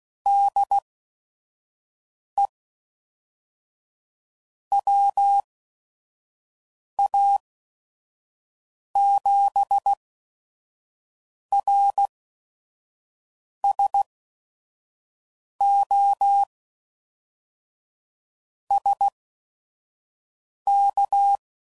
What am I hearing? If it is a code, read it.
Morse code, heard as a keyed tone, about 16 wpm, spaced as for 5 wpm, 800 Hz. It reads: DE WA7RSO SK
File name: CW Code Groups @5WPM / QSOs / kc7haa